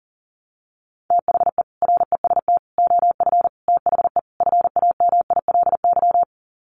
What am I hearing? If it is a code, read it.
Morse code, heard as a keyed tone, about 40 wpm, 700 Hz. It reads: THE REST OF THE FAMILY
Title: Sets of 5 Words Encore2 40wpm